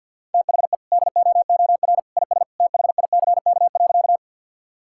Morse code, heard as a keyed tone, about 50 wpm, 700 Hz. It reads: THE DOOR IS THICK.
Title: Sentences from Top 1600 Words 50wpm